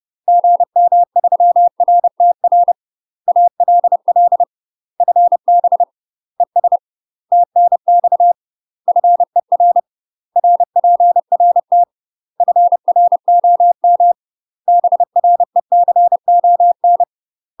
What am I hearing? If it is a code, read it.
Morse code, heard as a keyed tone, about 30 wpm, 700 Hz. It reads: GM3RTR ALL FB ES TNX FER RPRT FROM BRECON
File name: UK MASTER QSOs TRAINING / 30WPM / UK MASTER QSOs 0035